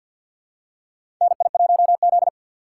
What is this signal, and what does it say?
Morse code, heard as a keyed tone, about 50 wpm, 700 Hz. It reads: NI1Z
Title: USA Call Signs - 4 Characters - 6x Character Spacing - 50wpm